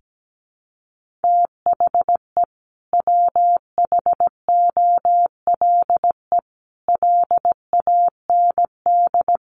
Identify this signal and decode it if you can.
Morse code, heard as a keyed tone, about 17 wpm, 700 Hz. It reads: THE WHOLE LAND
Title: Sentences from Top 1400 Words 17wpm